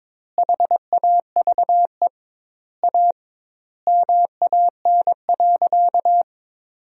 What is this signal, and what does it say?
Morse code, heard as a keyed tone, about 22 wpm, 700 Hz. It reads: HAVE A MAN.
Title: 2x Word-Spacing - Sentences from Top 300 Words - 22wpm